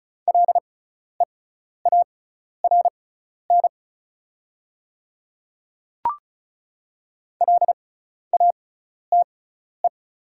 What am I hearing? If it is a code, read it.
Morse code, heard as a keyed tone, about 34 wpm, spaced as for 12 wpm, 690 Hz. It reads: LEARN LATE
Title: 6x Character Spacing - Top 300 Words - 35wpm